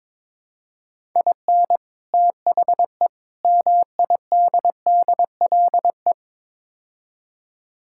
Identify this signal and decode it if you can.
Morse code, heard as a keyed tone, about 22 wpm, 700 Hz. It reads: IN THE MIDDLE
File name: practice / 22wpm / Build A Phrase - 5-Word Phrases - 22wpm